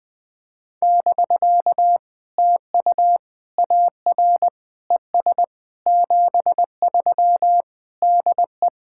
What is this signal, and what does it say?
Morse code, heard as a keyed tone, about 20 wpm, 700 Hz. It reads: <BK> TU AR ES 73 DE
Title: POTA QSO Practice - 20wpm